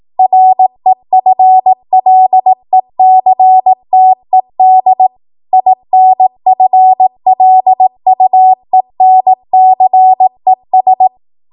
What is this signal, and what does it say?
Morse code, heard as a keyed tone, about 18 wpm, 750 Hz. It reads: REFLECTED INFLUENCES
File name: Morse / 18WPM